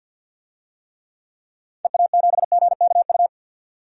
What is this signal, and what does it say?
Morse code, heard as a keyed tone, about 53 wpm, 700 Hz. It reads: EA7GKU
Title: DXCC-100 Call Signs - 6 Characters - 50wpm